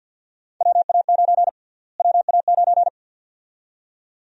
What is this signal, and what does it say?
Morse code, heard as a keyed tone, about 50 wpm, 700 Hz. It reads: WA9 WA9
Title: USA Call Sign Prefixes - Repeated 2x - 50wpm